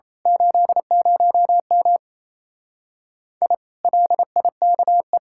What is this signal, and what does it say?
Morse code, heard as a keyed tone, about 27 wpm, 700 Hz. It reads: 80M I LIKE
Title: Speed-Racing Common QSO Phrases - 28wpm